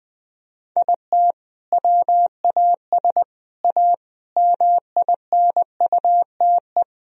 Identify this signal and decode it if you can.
Morse code, heard as a keyed tone, about 20 wpm, 700 Hz. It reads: IT WAS A MINUTE
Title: Sentences from Top 1000 Words - Review Part 1 - 20wpm